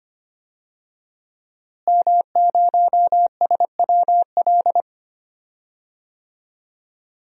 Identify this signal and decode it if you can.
Morse code, heard as a keyed tone, about 25 wpm, 700 Hz. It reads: M0SWL
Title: DXCC-100 Call Signs - 5 Characters - Repeated 3x - 25wpm